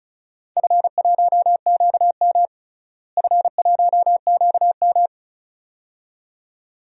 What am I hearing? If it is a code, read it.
Morse code, heard as a keyed tone, about 36 wpm, 700 Hz. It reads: F1QM F1QM
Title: DXCC-100 Call Signs - 4 Characters - Repeated 2x - 35wpm